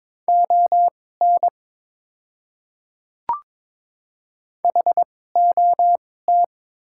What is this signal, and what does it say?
Morse code, heard as a keyed tone, about 22 wpm, spaced as for 16 wpm, 700 Hz. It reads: ON HOT
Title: 2x Character Spacing - Top 500 Words - 22wpm